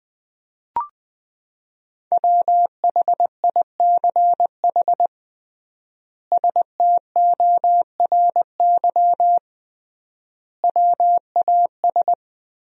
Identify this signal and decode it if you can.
Morse code, heard as a keyed tone, about 20 wpm, 700 Hz. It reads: WHICH STORY WAS
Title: 3x Word-Spacing - Sentences from Top 300 Words - 20wpm